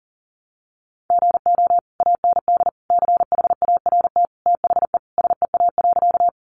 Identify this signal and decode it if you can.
Morse code, heard as a keyed tone, about 40 wpm, 700 Hz. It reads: GO AND CHART THE SEA.
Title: Sentences from Top 1300 Words 40wpm